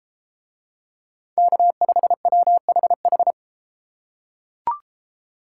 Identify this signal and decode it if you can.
Morse code, heard as a keyed tone, about 33 wpm, 700 Hz. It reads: K5WHH